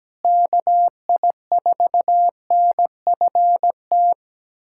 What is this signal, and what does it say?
Morse code, heard as a keyed tone, about 17 wpm, 700 Hz. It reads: KI4NFT
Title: USA Callsign DE Callsign - 6 Characters - 17wpm